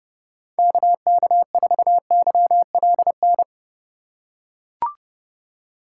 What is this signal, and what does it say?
Morse code, heard as a keyed tone, about 31 wpm, 700 Hz. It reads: KK4YLN
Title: USA Call Signs - 6 Characters - 2x Character Spacing - 30wpm